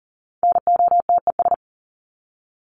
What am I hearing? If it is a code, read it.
Morse code, heard as a keyed tone, about 40 wpm, 700 Hz. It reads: NOTES
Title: ES:S Suffix - Top 100 Words with Base Words 40wpm